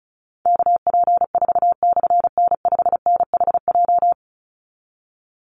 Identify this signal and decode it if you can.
Morse code, heard as a keyed tone, about 34 wpm, 700 Hz. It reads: KP4/N5NHJ
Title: Call Signs - 35wpm